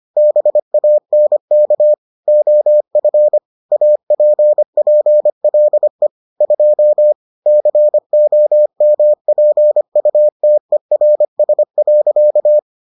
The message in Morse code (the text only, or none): BANK OF APPLE 2 COMPUTERS.